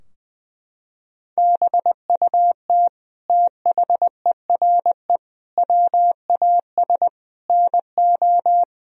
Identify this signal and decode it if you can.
Morse code, heard as a keyed tone, about 20 wpm, 700 Hz. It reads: BUT THERE WAS NO
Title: Sentences from Top 1000 Words - Review Part 2 - 20wpm